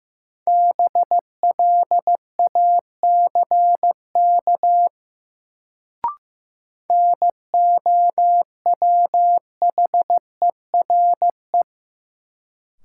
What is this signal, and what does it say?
Morse code, heard as a keyed tone, about 16 wpm, 700 Hz. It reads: BLACK NOWHERE
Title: Sets of 5 Words Encore - New Word Drill - 15wpm